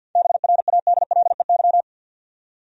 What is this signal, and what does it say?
Morse code, heard as a keyed tone, about 52 wpm, 700 Hz. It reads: BRADLEY
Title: USA Common Names - 50wpm